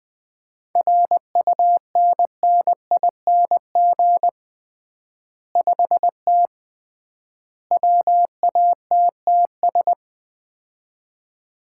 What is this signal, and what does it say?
Morse code, heard as a keyed tone, about 20 wpm, 700 Hz. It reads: RUNNING 5T WATTS
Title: Common QSO Phrases - 3x Word-Spacing - 20wpm